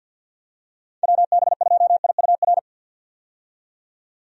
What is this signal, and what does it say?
Morse code, heard as a keyed tone, about 51 wpm, 700 Hz. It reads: WB2IUR
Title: USA Call Signs - 6 Characters - 50wpm